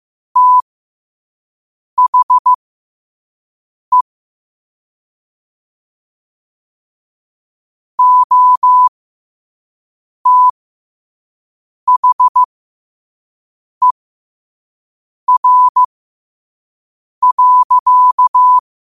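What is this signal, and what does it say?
Morse code, heard as a keyed tone, about 15 wpm, spaced as for 5 wpm, 1000 Hz. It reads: THE OTHER.
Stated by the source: Morse code at 5 words per minute.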